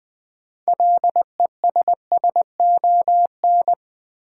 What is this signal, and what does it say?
Morse code, heard as a keyed tone, about 21 wpm, 700 Hz. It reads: LESSON
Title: Sentences from Top 1200 Words - New Word Drill - 20wpm